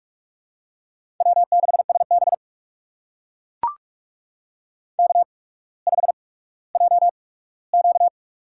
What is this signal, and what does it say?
Morse code, heard as a keyed tone, about 44 wpm, spaced as for 19 wpm, 700 Hz. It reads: W6SB K5JQ
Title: USA Call Signs - 4 Characters - 8x Character Spacing - 45wpm